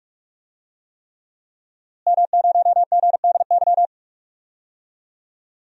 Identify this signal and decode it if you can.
Morse code, heard as a keyed tone, about 43 wpm, 700 Hz. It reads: M0GDY
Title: DXCC-100 Call Signs - 5 Characters - Repeated 3x - 45wpm